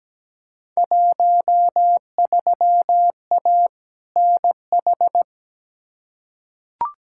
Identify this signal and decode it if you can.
Morse code, heard as a keyed tone, about 17 wpm, 700 Hz. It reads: 13A NH
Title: Field Day Exchanges Encore - 17wpm